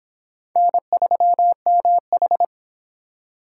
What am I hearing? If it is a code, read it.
Morse code, heard as a keyed tone, about 26 wpm, 700 Hz. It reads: N3MH